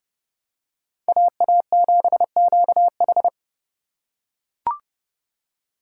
Morse code, heard as a keyed tone, about 28 wpm, 700 Hz. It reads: AA7QH